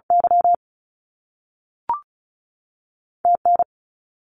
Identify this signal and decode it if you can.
Morse code, heard as a keyed tone, about 36 wpm, 700 Hz. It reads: Y T N
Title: US States 35wpm